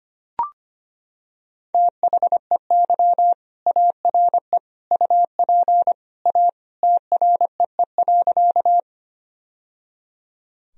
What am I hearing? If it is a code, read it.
Morse code, heard as a keyed tone, about 25 wpm, 700 Hz. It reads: THEY ARE UP A TREE.
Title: Sentences from Top 1000 Words - Review Part 1 - 25wpm